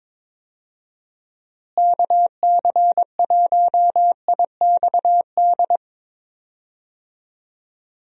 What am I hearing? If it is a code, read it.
Morse code, heard as a keyed tone, about 22 wpm, 700 Hz. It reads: KC1IXD
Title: USA Call Signs - 6 Characters - Repeated 3x - 22wpm